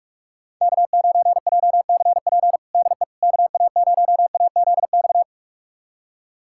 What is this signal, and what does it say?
Morse code, heard as a keyed tone, about 46 wpm, 690 Hz. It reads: K9JCP DE KA0AZX
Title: USA Callsign DE Callsign 45wpm